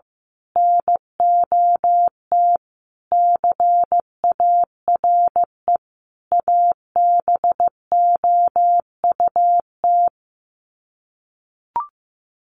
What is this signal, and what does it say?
Morse code, heard as a keyed tone, about 15 wpm, 700 Hz. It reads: NOT CARE ABOUT
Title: Sets of 5 Words Encore2 15wpm